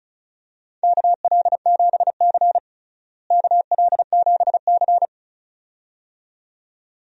Morse code, heard as a keyed tone, about 34 wpm, 700 Hz. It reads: KL7C KL7C
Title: DXCC-100 Call Signs - 4 Characters - Repeated 2x - 35wpm